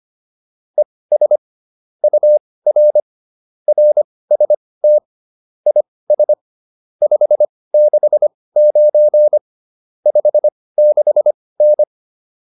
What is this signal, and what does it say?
Morse code, heard as a keyed tone, about 25 wpm, spaced as for 18 wpm, 600 Hz. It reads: ES UR RST IS 569 56N